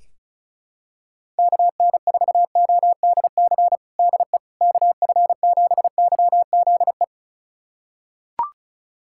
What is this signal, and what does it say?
Morse code, heard as a keyed tone, about 34 wpm, 700 Hz. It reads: KN4ODC DE KF7YZE
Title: USA Callsign DE Callsign 35wpm